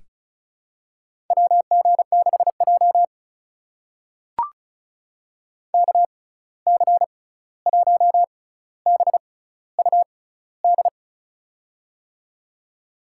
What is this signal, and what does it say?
Morse code, heard as a keyed tone, about 36 wpm, spaced as for 16 wpm, 700 Hz. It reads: WG6J KC1BUD
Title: USA Call Signs - 6x Character Spacing - 35wpm